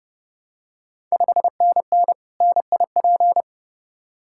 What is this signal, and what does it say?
Morse code, heard as a keyed tone, about 30 wpm, 700 Hz. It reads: 5NN NIP
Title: Ontario QSO Party Exchange 30wpm